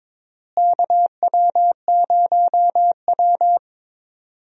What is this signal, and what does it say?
Morse code, heard as a keyed tone, about 22 wpm, 700 Hz. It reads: KW0W
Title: USA Call Signs - 4 Characters - Repeated 3x - 22wpm